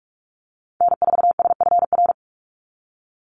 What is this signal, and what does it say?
Morse code, heard as a keyed tone, about 44 wpm, 700 Hz. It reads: N4SFR
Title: NA SOTA Activator Callsigns 45wpm